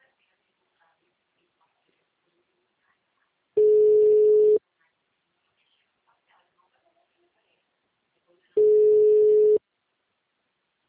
Ringback tone (Europe, 425 Hz).